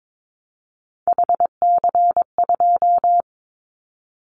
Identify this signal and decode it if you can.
Morse code, heard as a keyed tone, about 22 wpm, 700 Hz. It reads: HC2
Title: Callsign Prefixes for Top 100 DXCC - 22wpm